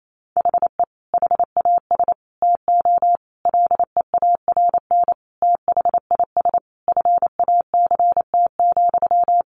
Morse code, heard as a keyed tone, about 28 wpm, 700 Hz. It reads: HE HAS TO LEARN THIS FACT,
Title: Sentences from Top 1200 Words 28wpm